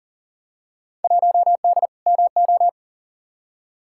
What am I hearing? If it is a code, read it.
Morse code, heard as a keyed tone, about 39 wpm, 700 Hz. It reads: 1D MO
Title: Field Day Exchanges - 40wpm